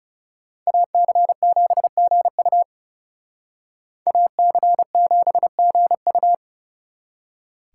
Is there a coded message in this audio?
Yes: Morse code from 0.7 to 6.3 s, heard as a keyed tone, about 38 wpm, 700 Hz. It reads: AC7GU AC7GU